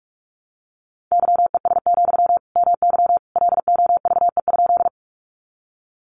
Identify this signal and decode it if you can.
Morse code, heard as a keyed tone, about 44 wpm, 700 Hz. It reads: YES, MY LOVE?
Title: Sentences from Top 500 Words 45wpm